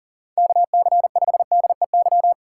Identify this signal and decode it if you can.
Morse code, heard as a keyed tone, about 38 wpm, 690 Hz. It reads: KC5DEY
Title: USA Call Signs - 6 Characters - 2x Character Spacing - 40wpm